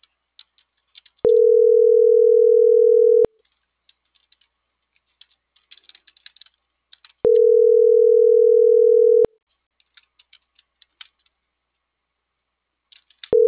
Ringback tone (North America), 440 + 480 Hz.